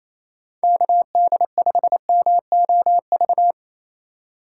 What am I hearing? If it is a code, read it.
Morse code, heard as a keyed tone, about 28 wpm, 700 Hz. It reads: KD5MOV
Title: USA Call Signs - 6 Characters - 3x Character Spacing - 28wpm